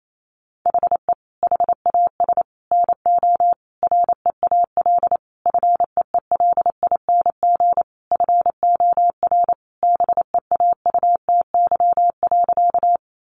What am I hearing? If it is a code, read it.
Morse code, heard as a keyed tone, about 28 wpm, 700 Hz. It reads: HE HAS NO REAL FEELING FOR BEAUTY.
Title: Sentences from Top 1200 Words 28wpm